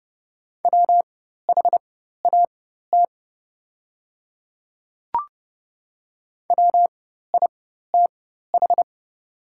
Morse code, heard as a keyed tone, about 30 wpm, spaced as for 14 wpm, 700 Hz. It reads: WHAT WITH
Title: 4x Character Spacing - Top 100 Words - 30wpm